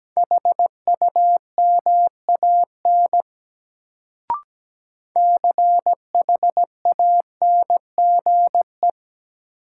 Morse code, heard as a keyed tone, about 17 wpm, 700 Hz. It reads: HUMAN CHANGE
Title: Sentences from Top 600 Words - New Word Drill - 17wpm